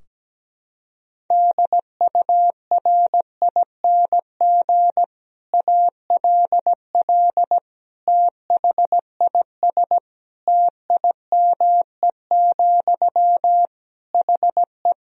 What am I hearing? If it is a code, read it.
Morse code, heard as a keyed tone, about 17 wpm, 700 Hz. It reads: DURING ALL THIS TIME, HE
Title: Sentences from Top 1000 Words - Review Part 3 - 17wpm